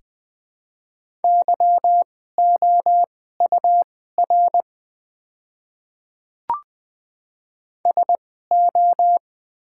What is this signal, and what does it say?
Morse code, heard as a keyed tone, about 20 wpm, spaced as for 15 wpm, 690 Hz. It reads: YOUR SO
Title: 2x Character Spacing - Top 100 Words - 20wpm